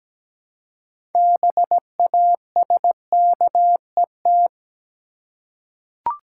Morse code, heard as a keyed tone, about 17 wpm, 700 Hz. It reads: BASKET
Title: Six Letter Words - 17wpm